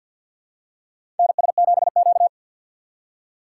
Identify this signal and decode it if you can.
Morse code, heard as a keyed tone, about 52 wpm, 700 Hz. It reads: NS7Q
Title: USA Call Signs - 4 Characters - 2x Character Spacing - 50wpm